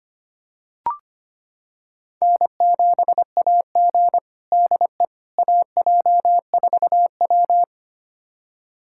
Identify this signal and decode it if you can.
Morse code, heard as a keyed tone, about 25 wpm, 700 Hz. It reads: N7AG DE AJ4W
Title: USA Callsign DE Callsign - 4 Characters - 25wpm